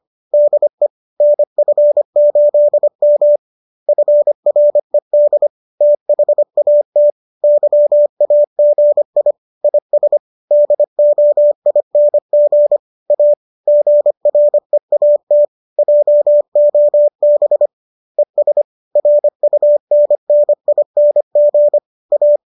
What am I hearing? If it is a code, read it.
Morse code, heard as a keyed tone, about 25 wpm, 590 Hz. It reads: DE NF8M FRED THAT YAGI IS DOING A GREAT JOB ES RUNNING A
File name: qso221 25